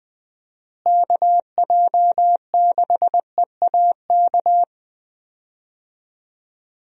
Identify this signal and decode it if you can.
Morse code, heard as a keyed tone, about 20 wpm, 700 Hz. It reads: KJ6EAK
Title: USA Call Signs - 6 Characters - Repeated 2x - 20wpm